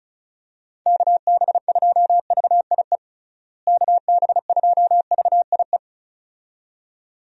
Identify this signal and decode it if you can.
Morse code, heard as a keyed tone, about 36 wpm, 700 Hz. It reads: KB2VIE KB2VIE
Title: USA Call Signs - 6 Characters - Repeated 2x - 35wpm